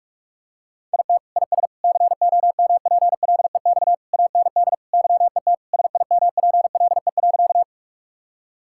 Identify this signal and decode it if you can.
Morse code, heard as a keyed tone, about 46 wpm, 700 Hz. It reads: IT IS COMPLEX AND YET SIMPLE.
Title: Sentences from Top 1000 Words - Review Part 6 - 45wpm